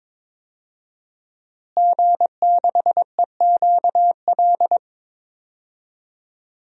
Morse code, heard as a keyed tone, about 22 wpm, 700 Hz. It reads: G6EQL
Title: DXCC-100 Call Signs - 5 Characters - Repeated 3x - 22wpm